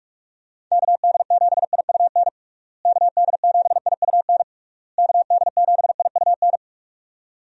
Morse code, heard as a keyed tone, about 41 wpm, 700 Hz. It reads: KD7IUN KD7IUN KD7IUN
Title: USA Call Signs - 6 Characters - Repeated 3x - 45wpm